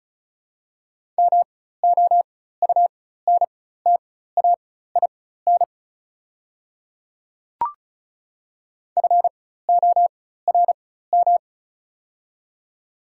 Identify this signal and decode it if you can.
Morse code, heard as a keyed tone, about 34 wpm, spaced as for 16 wpm, 700 Hz. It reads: MOUNTAIN FORM
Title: 4x Character Spacing - Top 300 Words - 35wpm